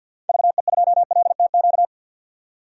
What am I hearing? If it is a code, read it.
Morse code, heard as a keyed tone, about 47 wpm, 700 Hz. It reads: VE2LTQ
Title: DXCC-100 Call Signs - 6 Characters - 50wpm